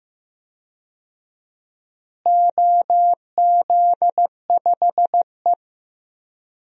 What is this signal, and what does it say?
Morse code, heard as a keyed tone, about 15 wpm, 700 Hz. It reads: OZ5E
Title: DXCC-100 Call Signs - 4 Characters - Repeated 3x - 15wpm